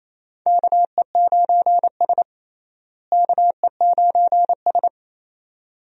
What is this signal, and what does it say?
Morse code, heard as a keyed tone, about 28 wpm, 700 Hz. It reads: KE9S KE9S
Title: USA Call Signs - 4 Characters - Repeated 2x - 28wpm